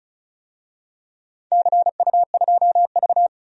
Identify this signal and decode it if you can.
Morse code, heard as a keyed tone, about 35 wpm, 700 Hz. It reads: CU2V
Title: DXCC-100 Call Signs - 4 Characters - 35wpm